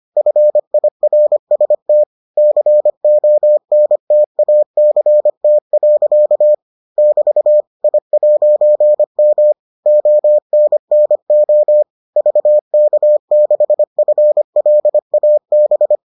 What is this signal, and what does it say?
Morse code, heard as a keyed tone, about 25 wpm, 600 Hz. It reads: FIRST CONTACT. <BT> I'M ONNO VK6FLAB